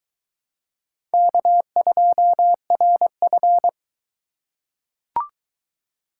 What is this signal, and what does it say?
Morse code, heard as a keyed tone, about 22 wpm, 700 Hz. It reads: K2RF